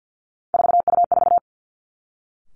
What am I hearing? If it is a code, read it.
Morse code, heard as a keyed tone, about 50 wpm, 700 Hz. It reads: 4U4
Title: Three Challenging Letter-Number Combinations 50wpm